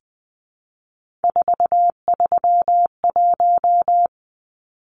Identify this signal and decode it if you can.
Morse code, heard as a keyed tone, about 20 wpm, 700 Hz. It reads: 431
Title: Numbers - 3-Digit 20wpm